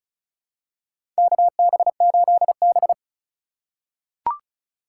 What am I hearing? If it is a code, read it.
Morse code, heard as a keyed tone, about 36 wpm, 700 Hz. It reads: KB8B
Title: USA Call Signs - 4 Characters - 3x Character Spacing - 35wpm